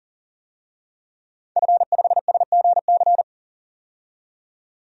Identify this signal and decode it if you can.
Morse code, heard as a keyed tone, about 40 wpm, 700 Hz. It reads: F5SGC
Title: DXCC-100 Call Signs - 5 Characters - 40wpm